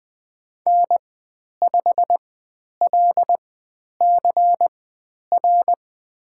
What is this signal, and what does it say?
Morse code, heard as a keyed tone, about 20 wpm, spaced as for 10 wpm, 700 Hz. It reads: N5LCR
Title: 102 Lesson 17g - Letter u - Call Signs 20wpm